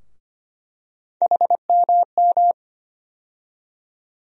Morse code, heard as a keyed tone, about 25 wpm, 700 Hz. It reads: HMM
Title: USA Call Sign Postfixes - 3 Letters 25wpm